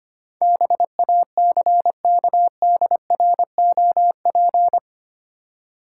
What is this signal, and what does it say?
Morse code, heard as a keyed tone, about 25 wpm, 700 Hz. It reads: BACKDROP
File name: 8 Letter Words - First 3 Letters Given - 25wpm